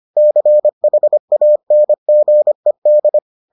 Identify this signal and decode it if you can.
Morse code, heard as a keyed tone, about 25 wpm, 600 Hz. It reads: CHANGED